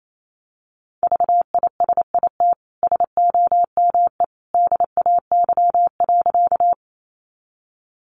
Morse code, heard as a keyed tone, about 28 wpm, 700 Hz. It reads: VISIT SOME DAY.
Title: Sentences from Top 900 Words 28wpm